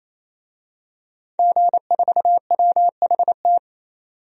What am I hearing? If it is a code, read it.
Morse code, heard as a keyed tone, about 28 wpm, 700 Hz. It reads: G4WHT
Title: DXCC-100 Call Signs - 5 Characters - 28wpm